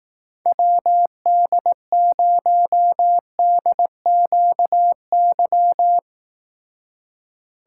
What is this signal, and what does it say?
Morse code, heard as a keyed tone, about 18 wpm, 700 Hz. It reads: WD0DQY
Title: Speed Racing - USA 2x3 Call Signs 15wpm